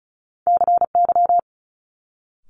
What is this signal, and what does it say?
Morse code, heard as a keyed tone, about 36 wpm, 700 Hz. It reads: CY
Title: Multi-letter Phonemes Quick Fire 35wpm